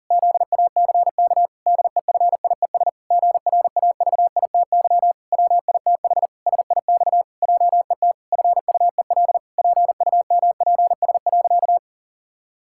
Morse code, heard as a keyed tone, about 37 wpm, 700 Hz. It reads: ZACK DEFIES GRAVITY WITH SIX JET FUEL PUMPS.
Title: Pangram Challenge v2 - 40wpm